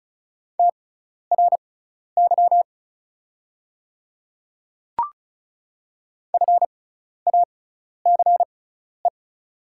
Morse code, heard as a keyed tone, about 36 wpm, spaced as for 12 wpm, 700 Hz. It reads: TRY FACE
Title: 6x Character Spacing - Top 300 Words - 35wpm